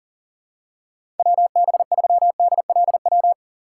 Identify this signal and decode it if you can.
Morse code, heard as a keyed tone, about 40 wpm, 700 Hz. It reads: WB3DLW
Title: USA Call Signs - 6 Characters - 3x Character Spacing - 40wpm